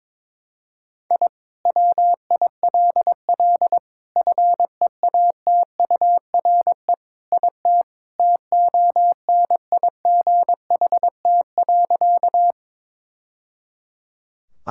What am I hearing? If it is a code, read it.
Morse code, heard as a keyed tone, about 22 wpm, 700 Hz. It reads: I WILL FEATURE IT TONIGHT.
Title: Sentences from Top 1000 Words - Review Part 5 - 22wpm